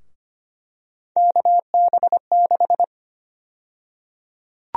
Morse code, heard as a keyed tone, about 25 wpm, 700 Hz. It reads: KB6
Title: USA Call Sign Prefixes - 25wpm